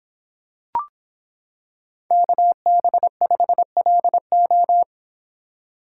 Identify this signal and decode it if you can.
Morse code, heard as a keyed tone, about 26 wpm, 700 Hz. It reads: KB5LO